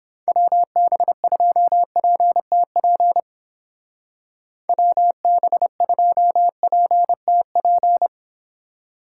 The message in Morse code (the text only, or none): WB2PTP WB2PTP